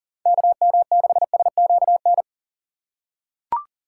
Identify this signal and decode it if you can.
Morse code, heard as a keyed tone, about 40 wpm, 700 Hz. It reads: KM6SQN
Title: USA Call Signs - 4x Character Spacing - 40wpm